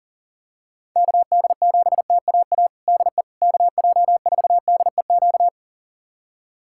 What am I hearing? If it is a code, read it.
Morse code, heard as a keyed tone, about 37 wpm, 700 Hz. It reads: KD7TAA DE KJ4DEQ
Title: USA Callsign DE Callsign 40wpm